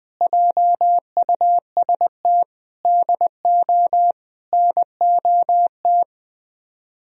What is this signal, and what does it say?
Morse code, heard as a keyed tone, about 20 wpm, 700 Hz. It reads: JUST DO NOT
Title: Speed-Racing - Sets of 4 Words 17wpm